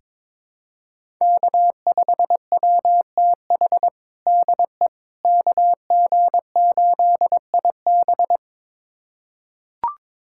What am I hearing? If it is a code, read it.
Morse code, heard as a keyed tone, about 22 wpm, 700 Hz. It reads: K5WTH DE KG8IB
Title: USA Callsign DE Callsign - 5 Characters - 22wpm